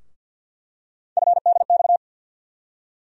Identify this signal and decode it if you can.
Morse code, heard as a keyed tone, about 47 wpm, 690 Hz. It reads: FDX
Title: USA Call Sign Postfixes - 3 Letters 50wpm